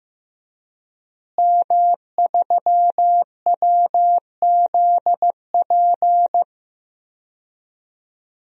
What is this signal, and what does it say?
Morse code, heard as a keyed tone, about 15 wpm, 700 Hz. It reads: M3WZP
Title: DXCC-100 Call Signs - 5 Characters - Repeated 2x - 15wpm